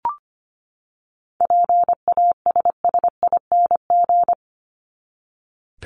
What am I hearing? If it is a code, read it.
Morse code, heard as a keyed tone, about 25 wpm, 700 Hz. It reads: PASSING
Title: ING Suffix - Top 100 Words with Base Words 25wpm